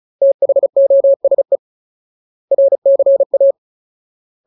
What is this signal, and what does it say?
Morse code, heard as a keyed tone, about 34 wpm, 550 Hz. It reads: THOSE RCA
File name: Ultimate Word List / 35WPM Ultimate